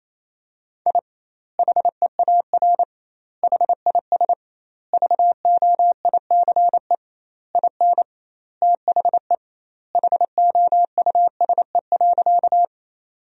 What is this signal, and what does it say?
Morse code, heard as a keyed tone, about 28 wpm, 700 Hz. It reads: I HEAR HIS VOICE IN THE HOUSE.
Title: 2x Word-Spacing - Sentences from Top 500 Words - 28wpm